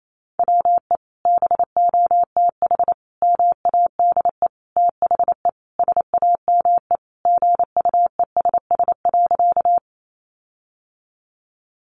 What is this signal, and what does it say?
Morse code, heard as a keyed tone, about 28 wpm, 700 Hz. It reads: WE BOTH MADE THE SAME GUESS.
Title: Sentences from Top 1500 Words 28wpm